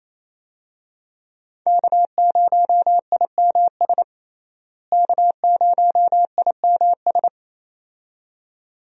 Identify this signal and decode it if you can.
Morse code, heard as a keyed tone, about 27 wpm, 700 Hz. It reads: K0IMS K0IMS
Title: USA Call Signs - 5 Characters - Repeated 2x - 28wpm